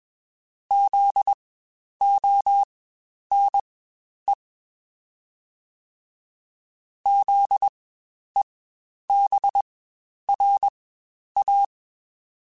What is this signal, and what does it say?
Morse code, heard as a keyed tone, about 21 wpm, spaced as for 10 wpm, 800 Hz. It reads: ZONE ZEBRA
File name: Spaced Alphas / Spaced Alphas mp3s / Zs sp @20